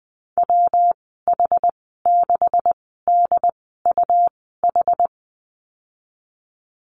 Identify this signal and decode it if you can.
Morse code, heard as a keyed tone, about 20 wpm, spaced as for 15 wpm, 700 Hz. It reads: WH6DUH
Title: 147 Lesson 24g - Letter y - Call Signs 20wpm